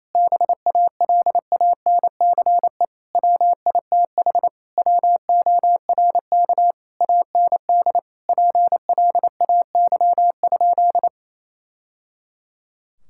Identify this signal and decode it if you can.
Morse code, heard as a keyed tone, about 28 wpm, 700 Hz. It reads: BALANCE WITH WORK AND PLAY?
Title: Sentences from Top 1000 Words - Review Part 6 - 28wpm